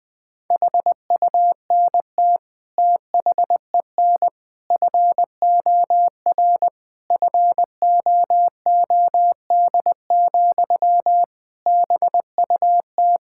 Morse code, heard as a keyed tone, about 20 wpm, 700 Hz. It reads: HUNT THEN FOR FOOD, BUT
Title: Sentences from Top 1600 Words 20wpm